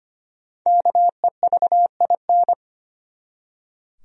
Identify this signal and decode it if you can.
Morse code, heard as a keyed tone, about 25 wpm, 700 Hz. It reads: KEVIN
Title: USA Common Names - Max 5 Letters - 25wpm